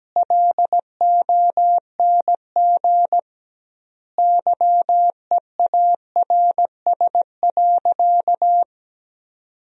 Morse code, heard as a keyed tone, about 17 wpm, 700 Hz. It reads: LONG YEARS.
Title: 2x Word-Spacing - Sentences from Top 200 Words - 17wpm